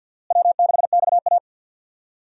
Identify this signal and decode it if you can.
Morse code, heard as a keyed tone, about 51 wpm, 700 Hz. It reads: W6XA
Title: USA Call Signs - 4 Characters - Repeated 3x - 50wpm